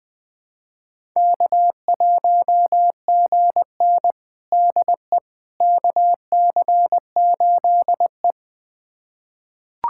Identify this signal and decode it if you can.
Morse code, heard as a keyed tone, about 20 wpm, 700 Hz. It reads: K1GN DE KC8E
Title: USA Callsign DE Callsign - 4 Characters - 20wpm